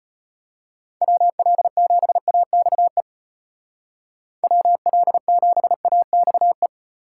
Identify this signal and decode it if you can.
Morse code, heard as a keyed tone, about 34 wpm, 700 Hz. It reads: WL7AXE WL7AXE